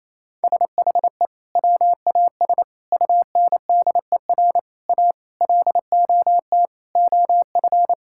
Morse code, heard as a keyed tone, about 27 wpm, 700 Hz. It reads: SHE WAS UNDER A LOT OF
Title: Sentences from Top 1000 Words - Review Part 6 - 28wpm